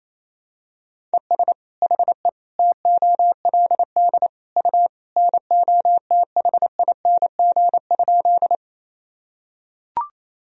Morse code, heard as a keyed tone, about 28 wpm, 700 Hz. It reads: ES HE TOLD U NOTHING?
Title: Sentences from Top 500 Words and Abbreviations 28wpm